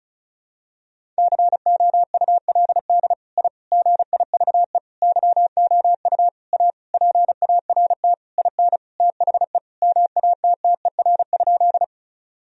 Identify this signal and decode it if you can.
Morse code, heard as a keyed tone, about 34 wpm, 700 Hz. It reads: COULD I GIVE YOU A PART IN THE MATTER?
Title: Sentences from Top 1000 Words - Review Part 1 - 35wpm